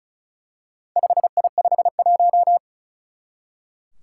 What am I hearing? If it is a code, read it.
Morse code, heard as a keyed tone, about 36 wpm, 690 Hz. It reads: 5I51
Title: Four Challenging Letter-Number Combinations 35wpm